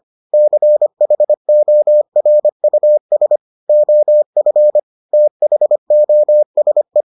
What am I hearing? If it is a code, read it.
Morse code, heard as a keyed tone, about 25 wpm, 600 Hz. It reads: CHORUS OF THOSE